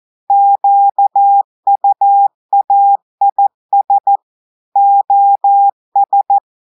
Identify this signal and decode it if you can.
Morse code, heard as a keyed tone, about 14 wpm, 800 Hz. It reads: QUAIS OS
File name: audios / portugais / portugais texte en clair 2 vitesse 14 mots minutes